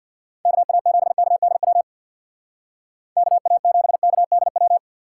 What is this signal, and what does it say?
Morse code, heard as a keyed tone, about 59 wpm, 700 Hz. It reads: KA7KDW KA7KDW
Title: Speed Racing - USA 2x3 Call Signs 50wpm